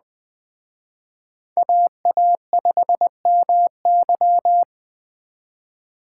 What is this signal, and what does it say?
Morse code, heard as a keyed tone, about 20 wpm, 700 Hz. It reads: AA5MY